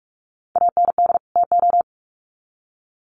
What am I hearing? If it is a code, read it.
Morse code, heard as a keyed tone, about 44 wpm, 700 Hz. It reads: AND TO
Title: Sets of 2 Words 45wpm